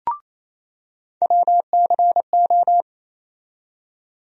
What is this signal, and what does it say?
Morse code, heard as a keyed tone, about 28 wpm, 700 Hz. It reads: WCO
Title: USA Call Sign Postfixes - 3 Letters 28wpm